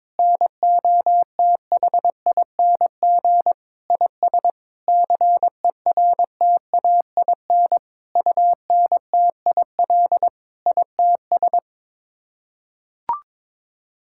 Morse code, heard as a keyed tone, about 22 wpm, 700 Hz. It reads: NOTHING IS CERTAIN UNTIL ITS
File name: practice / 22wpm / Sets of 8 Words - 22wpm